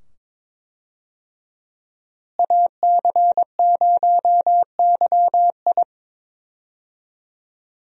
Morse code, heard as a keyed tone, about 22 wpm, 700 Hz. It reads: AC0YI
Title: USA Call Signs - 5 Characters - Repeated 2x - 22wpm